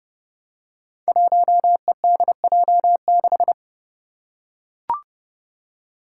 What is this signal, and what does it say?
Morse code, heard as a keyed tone, about 30 wpm, 700 Hz. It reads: 1EDJ6
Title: Five Challenging Letter-Number Combinations 30wpm